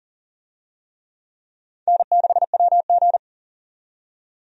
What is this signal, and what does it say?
Morse code, heard as a keyed tone, about 37 wpm, 700 Hz. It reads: N6WG
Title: USA Call Signs - 4 Characters - 40wpm